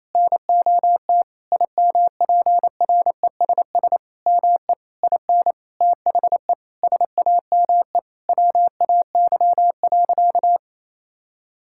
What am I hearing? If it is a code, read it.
Morse code, heard as a keyed tone, about 28 wpm, 700 Hz. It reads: NOT IMPRESS ME IN THE SAME WAY.
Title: Sentences from Top 1600 Words 28wpm